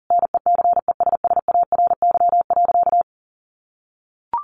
Morse code, heard as a keyed tone, about 37 wpm, 700 Hz. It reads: NECESSARY.
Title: Sentences from Top 1500 Words 40wpm